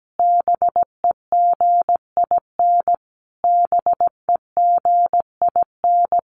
Morse code, heard as a keyed tone, about 17 wpm, 700 Hz. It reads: BEGIN BEGIN
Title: Top 500 Words Repeated 17wpm